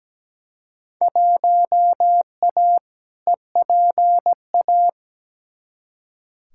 Morse code, heard as a keyed tone, about 17 wpm, 700 Hz. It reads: 1A EPA
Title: Field Day Exchanges - 17wpm